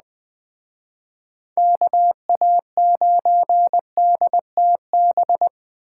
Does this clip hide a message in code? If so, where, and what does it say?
Yes: Morse code from 1.6 to 5.5 s, heard as a keyed tone, about 20 wpm, 700 Hz. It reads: KA9DTB